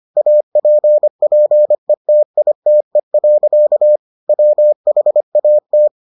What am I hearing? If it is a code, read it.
Morse code, heard as a keyed tone, about 25 wpm, 600 Hz. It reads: APPETITE. WHAT